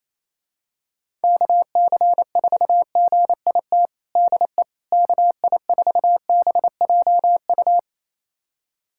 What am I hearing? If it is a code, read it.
Morse code, heard as a keyed tone, about 28 wpm, 700 Hz. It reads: KC4GIT DE KI4BJU
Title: USA Callsign DE Callsign 28wpm